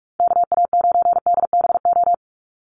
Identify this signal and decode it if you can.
Morse code, heard as a keyed tone, about 44 wpm, 690 Hz. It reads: KA9DBO
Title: USA Call Signs 45wpm